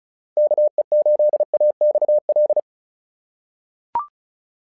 Morse code, heard as a keyed tone, about 36 wpm, 600 Hz. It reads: KE8AXL
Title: SOTA QSO Practice 35wpm